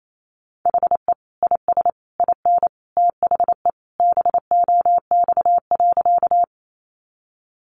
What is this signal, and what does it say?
Morse code, heard as a keyed tone, about 28 wpm, 700 Hz. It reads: HE IS IN THE BOX.
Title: Sentences from Top 500 Words 28wpm